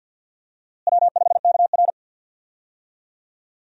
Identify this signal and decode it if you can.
Morse code, heard as a keyed tone, about 49 wpm, 700 Hz. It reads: W5KR
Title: USA Call Signs - 4 Characters - Repeated 2x - 50wpm